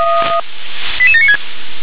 Five-tone selective call (ZVEI), heard as quick runs of five tones, tones of about 70 ms. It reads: 88796